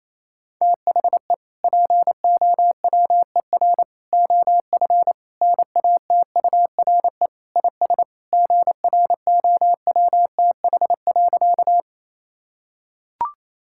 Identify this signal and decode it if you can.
Morse code, heard as a keyed tone, about 28 wpm, 700 Hz. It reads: THE POWER OF NATURE IS GROWTH.
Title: Sentences from Top 1000 Words - Review Part 5 - 28wpm